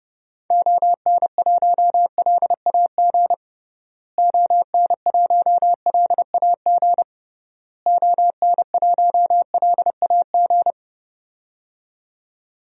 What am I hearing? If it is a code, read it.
Morse code, heard as a keyed tone, about 30 wpm, 700 Hz. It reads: ON1LAG ON1LAG ON1LAG
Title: DXCC-100 Call Signs - 6 Characters - Repeated 3x - 30wpm